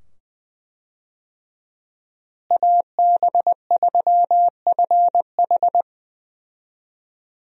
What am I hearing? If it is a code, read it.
Morse code, heard as a keyed tone, about 21 wpm, 700 Hz. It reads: AB3FH
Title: USA Call Signs - 5 Characters - Repeated 2x - 20wpm